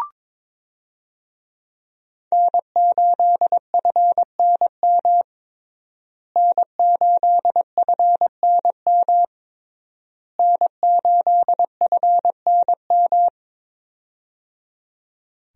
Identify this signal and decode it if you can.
Morse code, heard as a keyed tone, about 22 wpm, 700 Hz. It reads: N8FNM N8FNM N8FNM
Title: USA Call Signs - 5 Characters - Repeated 3x - 22wpm